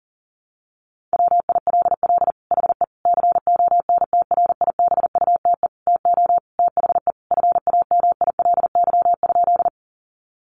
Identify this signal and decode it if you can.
Morse code, heard as a keyed tone, about 40 wpm, 700 Hz. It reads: WILL HE CONTRIBUTE TO THE FAMILY?
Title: Sentences from Top 1400 Words 40wpm